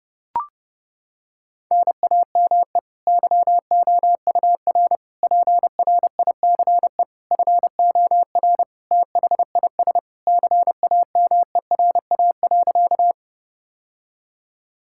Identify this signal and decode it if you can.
Morse code, heard as a keyed tone, about 30 wpm, 700 Hz. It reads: NAME YOUR PRICE FOR THIS CAMERA.
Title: Sentences from Top 1000 Words - Review Part 5 - 30wpm